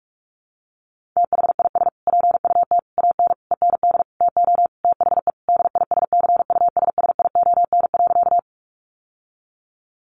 Morse code, heard as a keyed tone, about 44 wpm, 700 Hz. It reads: THIS PUT AN END TO THE DISCUSSION.
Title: Sentences from Top 1100 Words 45wpm